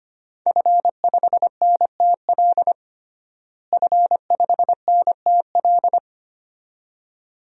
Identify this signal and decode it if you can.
Morse code, heard as a keyed tone, about 25 wpm, 700 Hz. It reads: F5NTL F5NTL
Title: DXCC-100 Call Signs - 5 Characters - Repeated 2x - 25wpm